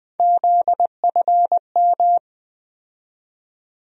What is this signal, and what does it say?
Morse code, heard as a keyed tone, about 20 wpm, 700 Hz. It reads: ZFM
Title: USA Call Sign Postfixes - 3 Letters 20wpm